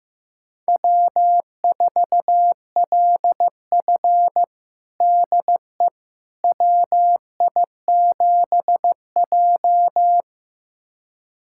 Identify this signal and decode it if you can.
Morse code, heard as a keyed tone, about 16 wpm, 700 Hz. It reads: W4LF DE WI7J
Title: USA Callsign DE Callsign - 4 Characters - 15wpm